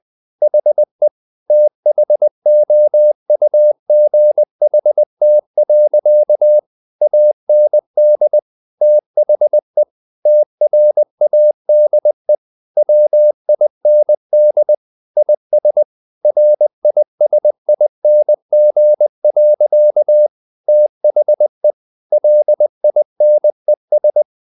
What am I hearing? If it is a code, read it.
Morse code, heard as a keyed tone, about 20 wpm, 600 Hz. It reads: HE THOUGHT. AND THE TRADE WIND IS RISING. THE LINES
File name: the old man and the sea part 2 20wpm